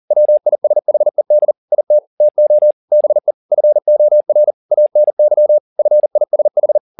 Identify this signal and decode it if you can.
Morse code, heard as a keyed tone, about 40 wpm, 600 Hz. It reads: WISHED IT TO BE FOR ANY FISH